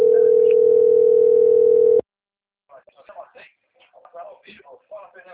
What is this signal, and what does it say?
Ringback tone (North America), 440 + 480 Hz, with speech.
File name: out- - - - - .80792